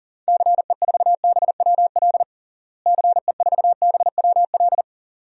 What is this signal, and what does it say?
Morse code, heard as a keyed tone, about 40 wpm, 700 Hz. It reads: CE4BWL CE4BWL
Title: DXCC-100 Call Signs - 6 Characters - Repeated 2x - 40wpm